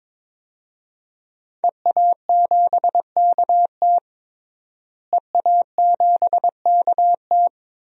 Morse code, heard as a keyed tone, about 21 wpm, 700 Hz. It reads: EA7KT EA7KT
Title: DXCC-100 Call Signs - 5 Characters - Repeated 3x - 22wpm